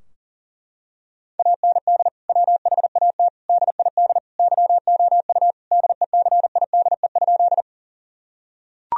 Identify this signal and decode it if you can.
Morse code, heard as a keyed tone, about 40 wpm, 700 Hz. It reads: AND WHAT DID YOU DECIDE?
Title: Sentences from Top 1000 Words - Review Part 3 - 40wpm